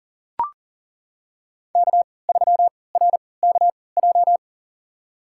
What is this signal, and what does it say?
Morse code, heard as a keyed tone, about 38 wpm, spaced as for 22 wpm, 700 Hz. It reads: K3RKJ
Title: USA Call Signs - 5 Characters - 3x Character Spacing - 40wpm